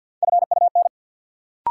Morse code, heard as a keyed tone, about 48 wpm, 700 Hz. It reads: FUN
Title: Sentences from Top 700 Words - New Word Drill - 50wpm